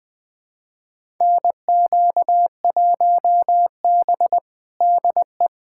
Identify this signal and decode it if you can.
Morse code, heard as a keyed tone, about 21 wpm, 700 Hz. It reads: NQ1B DE
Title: USA Callsign DE Callsign - 4 Characters - 20wpm